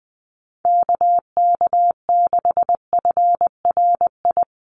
Morse code, heard as a keyed tone, about 20 wpm, 700 Hz. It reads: KK6FRI
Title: USA Call Signs Repeated 2x 20wpm